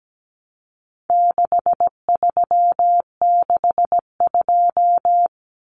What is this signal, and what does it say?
Morse code, heard as a keyed tone, about 17 wpm, 700 Hz. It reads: 6362
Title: Numbers - 4-Digit 17wpm